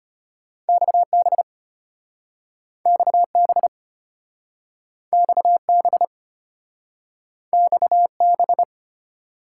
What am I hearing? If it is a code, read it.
Morse code, heard as a keyed tone, about 33 wpm, 700 Hz. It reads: XB XB XB XB